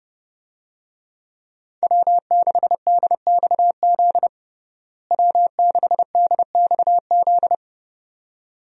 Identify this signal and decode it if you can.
Morse code, heard as a keyed tone, about 30 wpm, 700 Hz. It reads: W6DXZ W6DXZ
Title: USA Call Signs - 5 Characters - Repeated 2x - 30wpm